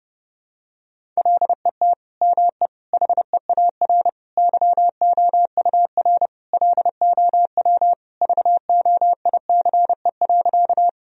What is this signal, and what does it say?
Morse code, heard as a keyed tone, about 30 wpm, 700 Hz. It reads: LET ME HEAR YOUR LOW VOICE.
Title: Sentences from Top 1000 Words - Review Part 1 - 30wpm